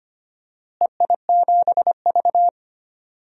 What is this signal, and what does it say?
Morse code, heard as a keyed tone, about 25 wpm, 700 Hz. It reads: EI7V
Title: DXCC-100 Call Signs - 4 Characters - Repeated 3x - 25wpm